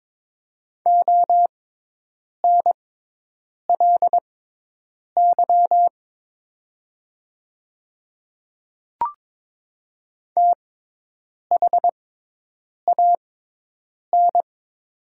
Morse code, heard as a keyed tone, about 22 wpm, spaced as for 8 wpm, 700 Hz. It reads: ONLY THAN
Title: 6x Character Spacing - Top 200 Words - 22wpm